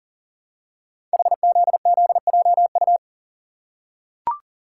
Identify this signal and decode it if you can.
Morse code, heard as a keyed tone, about 43 wpm, 700 Hz. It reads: HZZJU
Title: Code Groups - 5 Letters - 40wpm